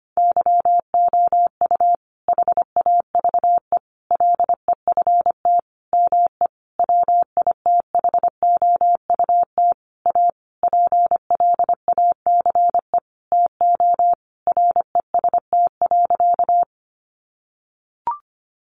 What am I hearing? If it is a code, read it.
Morse code, heard as a keyed tone, about 25 wpm, 700 Hz. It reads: YOU HAVE LEFT ME WITHOUT A PLACE TO REST.
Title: Sentences from Top 600 Words 25wpm